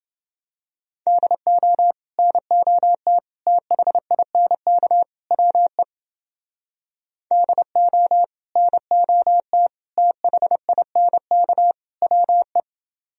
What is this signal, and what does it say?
Morse code, heard as a keyed tone, about 28 wpm, 700 Hz. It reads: DO NOT THINK WE DO NOT THINK WE